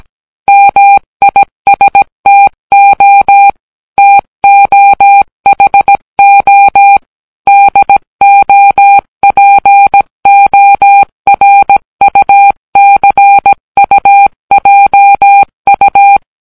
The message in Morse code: MISTO TOHO DOPORUCUJU